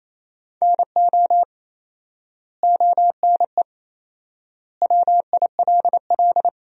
Morse code, heard as a keyed tone, about 28 wpm, 700 Hz. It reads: NO ONE WILL
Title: 4x Word-Spacing - Sentences from Top 100 Words - 28wpm